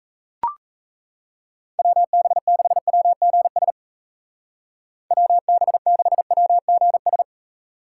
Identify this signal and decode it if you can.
Morse code, heard as a keyed tone, about 35 wpm, 700 Hz. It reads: WB6WGS WB6WGS
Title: Speed Racing - USA 2x3 Call Signs 28wpm